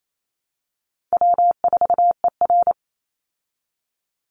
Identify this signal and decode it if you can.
Morse code, heard as a keyed tone, about 28 wpm, 700 Hz. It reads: W4ER
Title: Easy Call Signs Repeated 2x - 28wpm